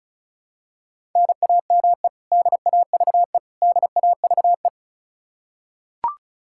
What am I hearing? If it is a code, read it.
Morse code, heard as a keyed tone, about 36 wpm, 700 Hz. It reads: NAME DAVE DAVE
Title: Practice QSO - Exchange 1 - Beginning - 35wpm